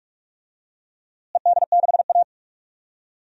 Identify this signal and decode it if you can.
Morse code, heard as a keyed tone, about 42 wpm, 700 Hz. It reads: ED6A
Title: DXCC-100 Call Signs - 4 Characters - Repeated 2x - 45wpm